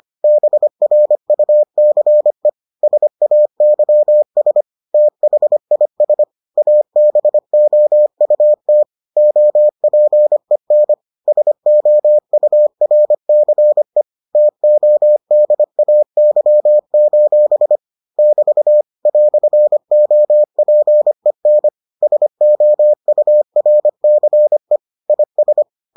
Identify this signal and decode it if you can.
Morse code, heard as a keyed tone, about 25 wpm, 600 Hz. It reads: BRUCE SAYS THIS ABOUT OPEN SOURCE TODAY: <BT> "OPEN SOURCE IS